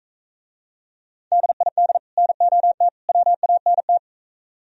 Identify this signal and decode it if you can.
Morse code, heard as a keyed tone, about 45 wpm, 700 Hz. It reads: DID NOT WANT